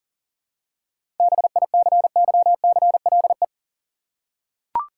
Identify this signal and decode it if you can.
Morse code, heard as a keyed tone, about 40 wpm, 700 Hz. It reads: BICYCLE
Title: Seven Letter Words - 40wpm